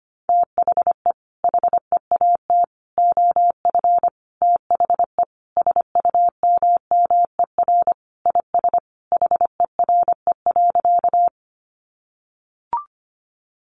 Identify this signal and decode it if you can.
Morse code, heard as a keyed tone, about 25 wpm, 700 Hz. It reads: THE HEAT OF THE SUMMER IS HERE.
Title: Sentences from Top 1100 Words 25wpm